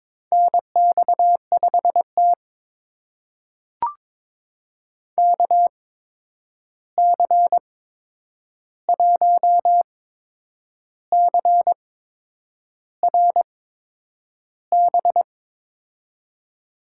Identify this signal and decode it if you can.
Morse code, heard as a keyed tone, about 22 wpm, spaced as for 8 wpm, 700 Hz. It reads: NX5T KC1CRB
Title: USA Call Signs - 8x Character Spacing - 22wpm